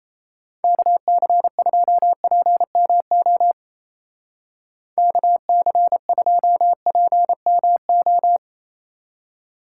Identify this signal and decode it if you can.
Morse code, heard as a keyed tone, about 35 wpm, 700 Hz. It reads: KC2PMO KC2PMO